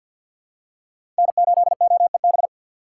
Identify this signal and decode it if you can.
Morse code, heard as a keyed tone, about 46 wpm, 700 Hz. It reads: N8OEB
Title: USA Call Signs - 4x Character Spacing - 50wpm